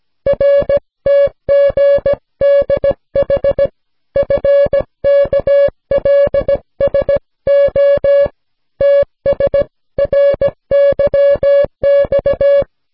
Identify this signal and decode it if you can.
Morse code, heard as a keyed tone, about 17 wpm, 560 Hz. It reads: RTGDH FKLSO TSRYX